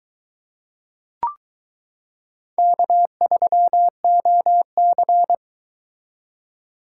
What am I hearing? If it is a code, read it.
Morse code, heard as a keyed tone, about 22 wpm, 700 Hz. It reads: K3OC